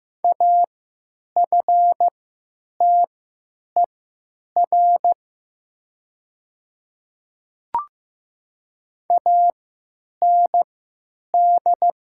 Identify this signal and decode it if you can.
Morse code, heard as a keyed tone, about 15 wpm, spaced as for 8 wpm, 700 Hz. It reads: AFTER AND
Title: 3x Character Spacing - Top 300 Words - 15wpm